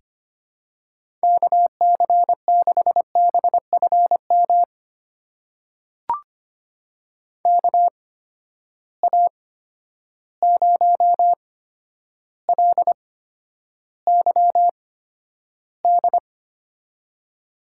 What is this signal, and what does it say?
Morse code, heard as a keyed tone, about 25 wpm, spaced as for 11 wpm, 700 Hz. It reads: KC6BFM KA0LYD
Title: USA Call Signs - 6 Characters - 8x Character Spacing - 25wpm